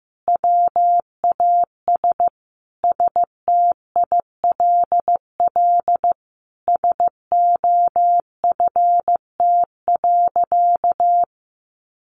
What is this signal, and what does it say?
Morse code, heard as a keyed tone, about 15 wpm, 700 Hz. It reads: WAS STILL SOFT.
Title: Sentences from Top 1500 Words 15wpm